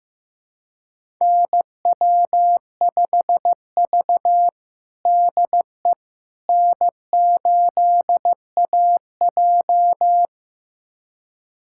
Morse code, heard as a keyed tone, about 15 wpm, 700 Hz. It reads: NW5V DE N8AJ
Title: USA Callsign DE Callsign - 4 Characters - 15wpm